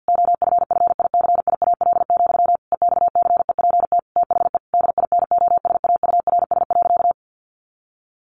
Morse code, heard as a keyed tone, about 50 wpm, 700 Hz. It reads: OFFICIAL, EXCEPT THE DINOSAURS.
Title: A Sound Of Thunder 50wpm